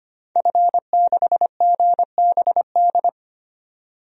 Morse code, heard as a keyed tone, about 25 wpm, 700 Hz. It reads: F6GBD
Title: DXCC-100 Call Signs - 5 Characters - 25wpm